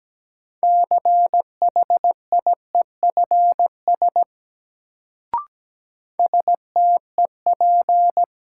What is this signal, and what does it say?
Morse code, heard as a keyed tone, about 17 wpm, 700 Hz. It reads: CHIEFS STEP
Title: Sets of 5 Words Encore - New Word Drill - 17wpm